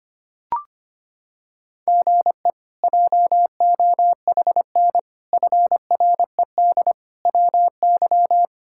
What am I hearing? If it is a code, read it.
Morse code, heard as a keyed tone, about 25 wpm, 700 Hz. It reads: GE JOHN FRED WY
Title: K1USN SST QSO Practice - 25wpm